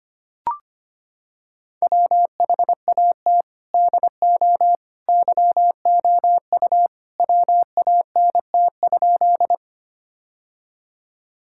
Morse code, heard as a keyed tone, about 25 wpm, 700 Hz. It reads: WHAT DO YOU WANT?
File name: practice / 25wpm / Sentences with 2-Word Phrases - 25wpm